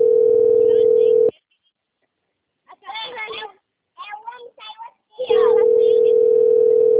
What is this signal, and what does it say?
Ringback tone (North America), 440 + 480 Hz, with speech.
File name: rg-1600-unknown- - - .33200